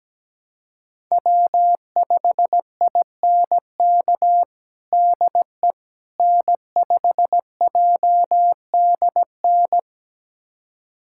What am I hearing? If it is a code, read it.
Morse code, heard as a keyed tone, about 17 wpm, 700 Hz. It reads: W5INK DE N5JDN
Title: USA Callsign DE Callsign - 5 Characters - 17wpm